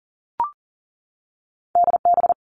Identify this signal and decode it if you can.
Morse code, heard as a keyed tone, about 42 wpm, 700 Hz. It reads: DB
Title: Two Challenging Letter-Number Combinations 40wpm